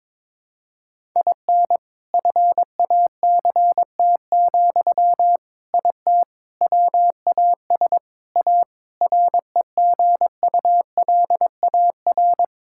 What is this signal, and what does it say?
Morse code, heard as a keyed tone, about 21 wpm, 700 Hz. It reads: IN FACT, IT WAS A REGULAR
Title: Sentences from Top 1000 Words - Review Part 6 - 22wpm